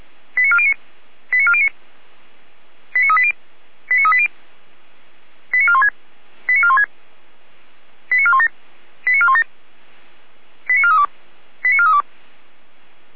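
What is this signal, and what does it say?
Five-tone selective call (ZVEI), heard as quick runs of five tones, tones of about 70 ms. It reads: 89409 89409 89380 89380 89416 89416 89417 89417 89432 89432
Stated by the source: ZVEI five-tone selective call 89361 received by SDR on channel path 455, logged by the Feuerwehr Orlamünde audio server.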